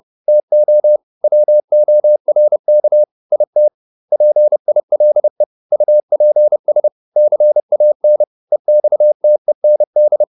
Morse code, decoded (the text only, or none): TO WORK IT PILE UPS CAN EXTEND